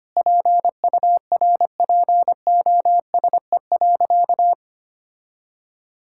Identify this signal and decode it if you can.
Morse code, heard as a keyed tone, about 25 wpm, 700 Hz. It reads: PURPOSE.
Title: Sentences from Top 1000 Words - Review Part 5 - 25wpm